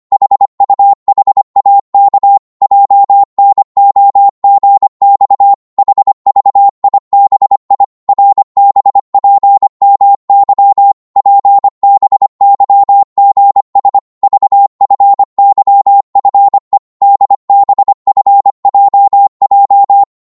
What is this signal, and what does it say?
Morse code, heard as a keyed tone, about 25 wpm, 800 Hz. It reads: HUHAK JNOGX HVIBI RBPMY PBYGS VFYFE DBFJJ
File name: Morse / ltr-25wpm